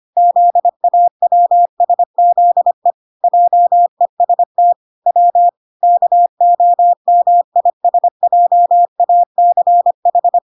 Morse code, heard as a keyed tone, about 25 wpm, 700 Hz. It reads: ZAWSZE JEST W KOMISJACH